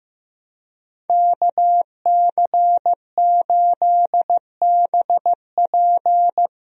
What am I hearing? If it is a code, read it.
Morse code, heard as a keyed tone, about 16 wpm, 700 Hz. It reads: KC8BP
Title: USA Call Signs - 5 Characters - Repeated 2x - 15wpm